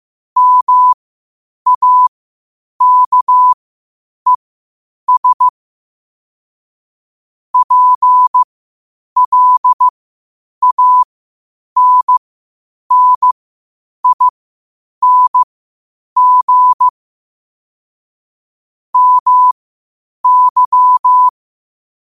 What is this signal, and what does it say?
Morse code, heard as a keyed tone, about 15 wpm, spaced as for 8 wpm, 1000 Hz. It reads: MAKES PLANNING MY
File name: qotdfiles / 8wpm